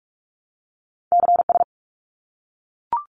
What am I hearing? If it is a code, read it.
Morse code, heard as a keyed tone, about 44 wpm, 700 Hz. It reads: CS
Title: Abbreviations 45wpm